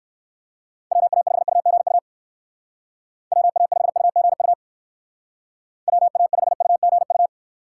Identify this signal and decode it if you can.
Morse code, heard as a keyed tone, about 59 wpm, 700 Hz. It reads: WA5UGU WA5UGU WA5UGU